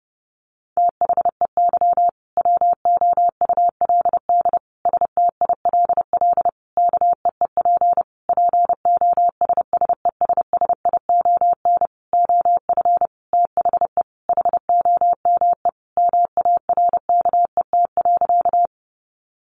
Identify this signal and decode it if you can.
Morse code, heard as a keyed tone, about 30 wpm, 700 Hz. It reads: THEY WOULD STILL KEEP POSSESSION OF THE HOME MARKET.
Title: Sentences from Top 1400 Words 30wpm